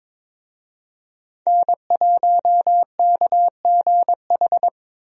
Morse code, heard as a keyed tone, about 22 wpm, 700 Hz. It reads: N1KGH
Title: USA Call Signs - 5 Characters - Repeated 2x - 22wpm